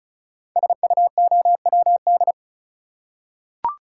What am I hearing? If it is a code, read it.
Morse code, heard as a keyed tone, about 36 wpm, 690 Hz. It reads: SUOWD
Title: Code Groups - 5 Letters - 35wpm